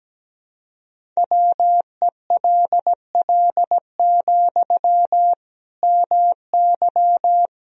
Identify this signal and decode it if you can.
Morse code, heard as a keyed tone, about 17 wpm, 700 Hz. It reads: WELL, MY
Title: Sentences from Top 1000 Words - Review Part 2 - 17wpm